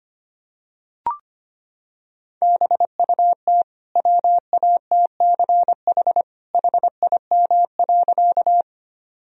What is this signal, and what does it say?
Morse code, heard as a keyed tone, about 25 wpm, 700 Hz. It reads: BUT WATCH HIM.
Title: Sentences from Top 1000 Words - Review Part 1 - 25wpm